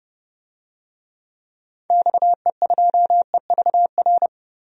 Morse code, heard as a keyed tone, about 31 wpm, 700 Hz. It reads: XE2EVR
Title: DXCC-100 Call Signs - 6 Characters - 30wpm